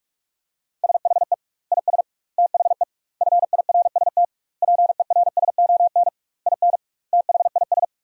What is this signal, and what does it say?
Morse code, heard as a keyed tone, about 46 wpm, 700 Hz. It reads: SHE IS THE FIRST PERSON IN THIS
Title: 2x Word-Spacing - Sentences from Top 400 Words - 45wpm